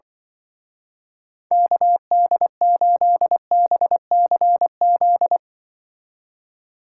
Morse code, heard as a keyed tone, about 24 wpm, 700 Hz. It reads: KD8BCZ
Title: Speed Racing - USA Call Signs 20wpm